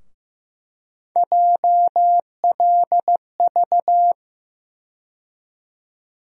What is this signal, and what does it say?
Morse code, heard as a keyed tone, about 15 wpm, 700 Hz. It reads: JLV
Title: USA Call Sign Postfixes - 3 Letters 15wpm